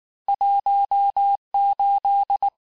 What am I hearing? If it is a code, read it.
Morse code, heard as a keyed tone, about 19 wpm, 800 Hz. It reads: 18
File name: cours-du-lundi files / cours-1595-teste-PONCTUATIONS-vitesse18-MPM-puis-23-MPM